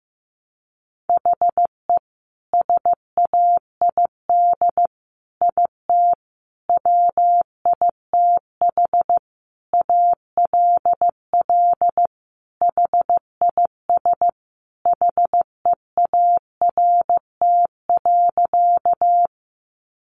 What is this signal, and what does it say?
Morse code, heard as a keyed tone, about 14 wpm, 700 Hz. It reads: HE SAID IT WITH ALL HIS HEART.
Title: Sentences from Top 1200 Words 15wpm